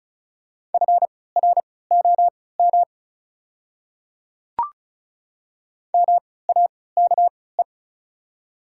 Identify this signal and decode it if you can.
Morse code, heard as a keyed tone, about 36 wpm, spaced as for 20 wpm, 690 Hz. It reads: FROM MAKE
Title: 3x Character Spacing - Top 100 Words - 35wpm